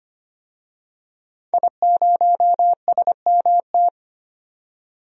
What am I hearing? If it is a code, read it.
Morse code, heard as a keyed tone, about 25 wpm, 700 Hz. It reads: I0SMT
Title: DXCC-100 Call Signs - 5 Characters - 25wpm